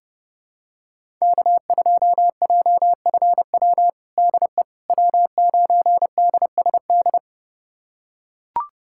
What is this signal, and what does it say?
Morse code, heard as a keyed tone, about 32 wpm, 700 Hz. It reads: K2JFW DE W9DSD
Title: USA Callsign DE Callsign - 5 Characters - 30wpm